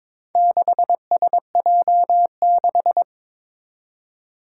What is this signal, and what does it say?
Morse code, heard as a keyed tone, about 22 wpm, 700 Hz. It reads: 6SJ6
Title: Four Challenging Letter-Number Combinations 22wpm